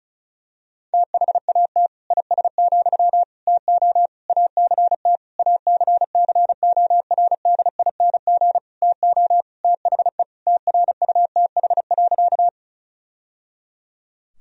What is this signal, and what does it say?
Morse code, heard as a keyed tone, about 36 wpm, 700 Hz. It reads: THAT IS, TO ACT ACCORDING TO THE TRUTH.
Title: Sentences from Top 1000 Words - Review Part 4 - 35wpm